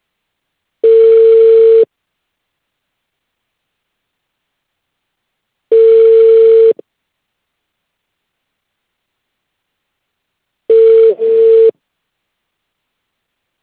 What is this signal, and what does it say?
Ringback tone (China, 450 Hz).